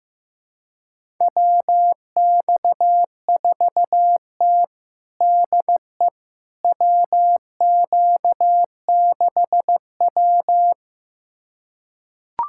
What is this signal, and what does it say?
Morse code, heard as a keyed tone, about 16 wpm, 700 Hz. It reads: WX4T DE WQ6W
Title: USA Callsign DE Callsign - 4 Characters - 15wpm